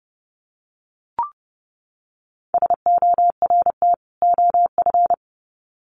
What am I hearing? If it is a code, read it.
Morse code, heard as a keyed tone, about 30 wpm, 700 Hz. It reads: SORT OF
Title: Sets of 2 Words 30wpm